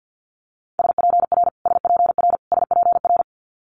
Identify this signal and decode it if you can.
Morse code, heard as a keyed tone, about 50 wpm, 700 Hz. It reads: SPR SPR SPR
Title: Multi-letter Phonemes Repeated 3x 50wpm